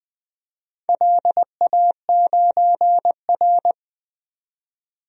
Morse code, heard as a keyed tone, about 20 wpm, 700 Hz. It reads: LA9R
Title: DXCC-100 Call Signs - 4 Characters - 20wpm